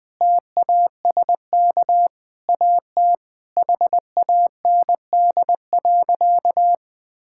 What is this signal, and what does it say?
Morse code, heard as a keyed tone, about 20 wpm, 700 Hz. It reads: TASK AT HAND.
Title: Sentences from Top 1600 Words 20wpm